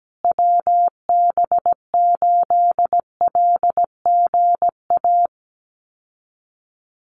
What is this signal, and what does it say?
Morse code, heard as a keyed tone, about 17 wpm, 700 Hz. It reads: WB8LGA
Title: USA Call Signs Repeated 2x 17wpm